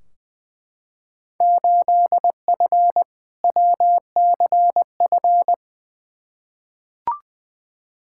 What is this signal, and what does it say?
Morse code, heard as a keyed tone, about 20 wpm, 700 Hz. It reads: 8F WCF
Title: Field Day Exchanges Encore - 20wpm